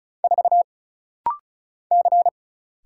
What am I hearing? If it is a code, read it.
Morse code, heard as a keyed tone, about 36 wpm, 700 Hz. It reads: 4 C
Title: Single Letter-Number Alt - Rapid-Fire 35wpm